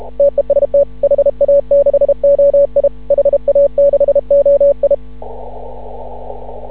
Morse code, heard as a keyed tone, about 32 wpm, 560 Hz. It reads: TEST HA6OI HA6OI
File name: CW / ha6oi 2